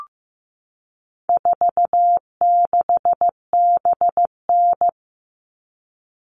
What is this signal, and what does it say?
Morse code, heard as a keyed tone, about 15 wpm, 700 Hz. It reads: 46BN
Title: All Cut Numbers - 4-Digit 15wpm